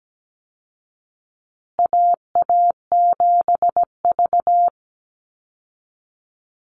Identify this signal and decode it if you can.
Morse code, heard as a keyed tone, about 17 wpm, 700 Hz. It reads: AA7V
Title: Call Signs Repeated 2x - 17wpm